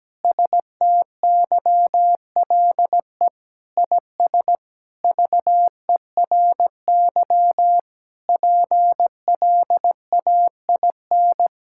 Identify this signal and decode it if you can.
Morse code, heard as a keyed tone, about 17 wpm, 700 Hz. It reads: STYLE IS VERY PLAIN
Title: Sentences from Top 1000 Words - Review Part 5 - 17wpm